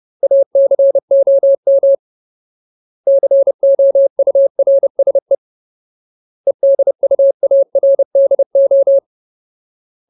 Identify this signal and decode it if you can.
Morse code, heard as a keyed tone, about 30 wpm, 550 Hz. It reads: ACOM COURSE EDUARDO
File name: Ultimate Word List / 30WPM Ultimate